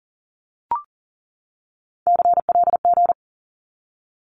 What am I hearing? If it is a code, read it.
Morse code, heard as a keyed tone, about 40 wpm, 700 Hz. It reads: CLG
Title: Abbreviations 40wpm